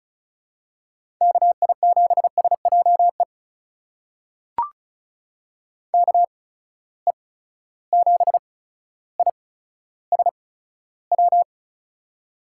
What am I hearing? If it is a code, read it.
Morse code, heard as a keyed tone, about 35 wpm, spaced as for 15 wpm, 700 Hz. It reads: KI7SJE KE7ISW
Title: USA Call Signs - 6 Characters - 8x Character Spacing - 35wpm